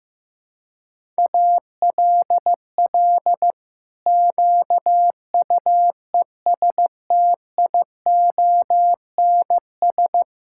Morse code, heard as a keyed tone, about 16 wpm, 700 Hz. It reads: ALL QUESTIONS
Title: Pangram Challenge v2 - 15wpm